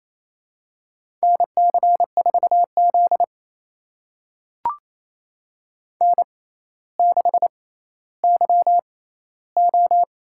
Morse code, heard as a keyed tone, about 28 wpm, spaced as for 14 wpm, 700 Hz. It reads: NC4Z N6YO
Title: USA Call Signs - 4 Characters - 6x Character Spacing - 28wpm